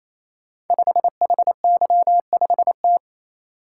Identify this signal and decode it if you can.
Morse code, heard as a keyed tone, about 27 wpm, 700 Hz. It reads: 5HY5T
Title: Code Groups - 5 Letter:Numbers - 28wpm